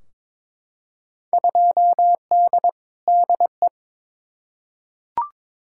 Morse code, heard as a keyed tone, about 22 wpm, 700 Hz. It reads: 2D DE
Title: Field Day Exchanges Encore - 22wpm